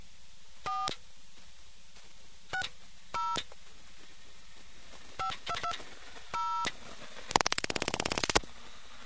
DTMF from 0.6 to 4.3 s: 4 60